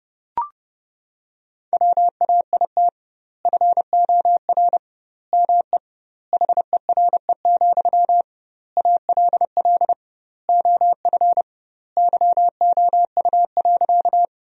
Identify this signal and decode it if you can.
Morse code, heard as a keyed tone, about 30 wpm, 700 Hz. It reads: WAIT FOR ME HERE, ALL OF YOU.
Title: 2x Word-Spacing - Sentences from Top 500 Words - 30wpm